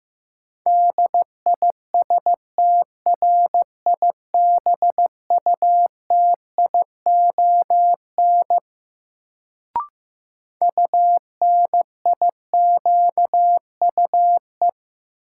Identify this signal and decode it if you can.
Morse code, heard as a keyed tone, about 16 wpm, 700 Hz. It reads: DISTRIBUTION UNIQUE
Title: Sentences from Top 1000 Words - Review Part 7 - New Word Drill - 15wpm